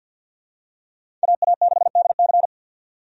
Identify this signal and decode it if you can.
Morse code, heard as a keyed tone, about 46 wpm, 700 Hz. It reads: AA6DC
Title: USA Call Signs - 5 Characters - 50wpm